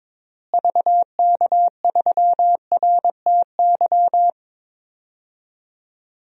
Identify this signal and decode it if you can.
Morse code, heard as a keyed tone, about 22 wpm, 700 Hz. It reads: VK3RTY
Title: DXCC-100 Call Signs - 6 Characters - Repeated 3x - 22wpm